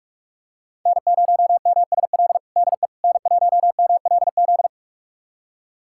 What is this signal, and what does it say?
Morse code, heard as a keyed tone, about 46 wpm, 700 Hz. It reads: N0MSL DE N1MLZ
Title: USA Callsign DE Callsign - 5 Characters - 45wpm